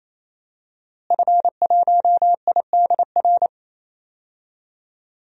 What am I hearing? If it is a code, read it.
Morse code, heard as a keyed tone, about 28 wpm, 700 Hz. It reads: F1IDR
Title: DXCC-100 Call Signs - 5 Characters - Repeated 3x - 28wpm